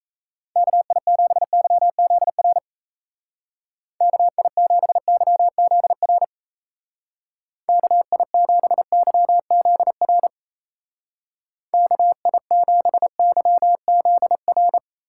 Morse code, heard as a keyed tone, about 34 wpm, 700 Hz. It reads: KI7YZR KI7YZR KI7YZR KI7YZR
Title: Speed Racing - USA 2x3 Call Signs 28wpm